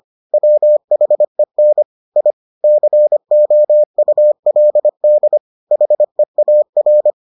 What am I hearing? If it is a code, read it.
Morse code, heard as a keyed tone, about 25 wpm, 600 Hz. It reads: WHEN I COULD HEAR